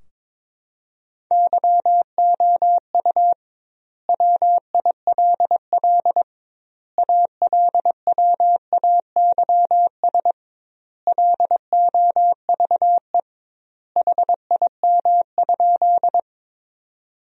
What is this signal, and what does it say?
Morse code, heard as a keyed tone, about 22 wpm, 700 Hz. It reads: YOU WILL ALWAYS LOVE HIM?
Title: 2x Word-Spacing - Sentences from Top 400 Words - 22wpm